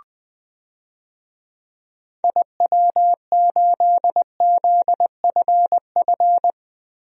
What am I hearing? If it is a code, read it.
Morse code, heard as a keyed tone, about 20 wpm, 700 Hz. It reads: IW8ZFF
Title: DXCC-100 Call Signs - 6 Characters - 20wpm